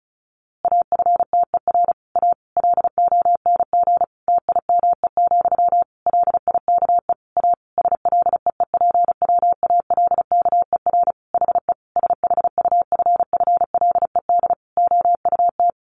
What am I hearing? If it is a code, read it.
Morse code, heard as a keyed tone, about 34 wpm, 700 Hz. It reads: AFTER A LONG TIME, LIKE A SLEEPWALKER HE SHUFFLED OUT
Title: A Sound Of Thunder 35wpm